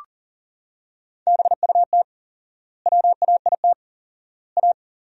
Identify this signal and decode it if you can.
Morse code, heard as a keyed tone, about 40 wpm, 700 Hz. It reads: BUT WAIT A
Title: 4x Word-Spacing - Sentences from Top 500 Words - 40wpm